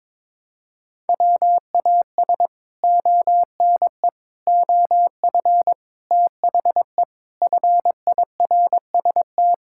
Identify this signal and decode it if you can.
Morse code, heard as a keyed tone, about 22 wpm, 700 Hz. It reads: WAS ONE OF THE FIRST
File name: practice / 22wpm / Build A Phrase - 5-Word Phrases - 22wpm